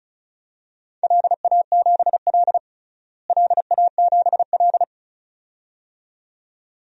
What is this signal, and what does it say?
Morse code, heard as a keyed tone, about 36 wpm, 700 Hz. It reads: LA7L LA7L
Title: DXCC-100 Call Signs - 4 Characters - Repeated 2x - 35wpm